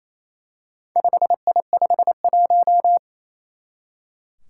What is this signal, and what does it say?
Morse code, heard as a keyed tone, about 27 wpm, 700 Hz. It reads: 5I51
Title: Four Challenging Letter-Number Combinations 28wpm